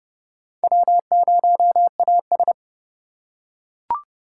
Morse code, heard as a keyed tone, about 30 wpm, 700 Hz. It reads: W0AS